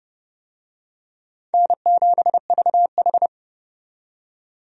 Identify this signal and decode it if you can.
Morse code, heard as a keyed tone, about 30 wpm, 700 Hz. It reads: N7VH
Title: USA Call Signs - 4 Characters - 30wpm